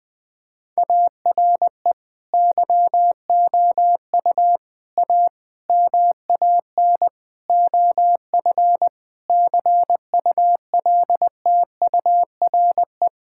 Morse code, heard as a keyed tone, about 19 wpm, 700 Hz. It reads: ARE YOU A MAN OF CULTURE
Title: Sentences from Top 1000 Words - Review Part 6 - 20wpm